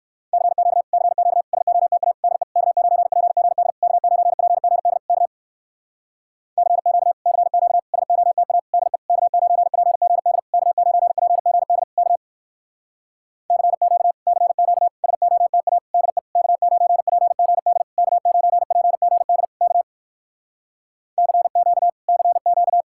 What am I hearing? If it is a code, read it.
Morse code, heard as a keyed tone, about 59 wpm, 700 Hz. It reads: CQ CQ SOTA DE K9PGD K9PGD K CQ CQ SOTA DE K9PGD K9PGD K CQ CQ SOTA DE K9PGD K9PGD K CQ CQ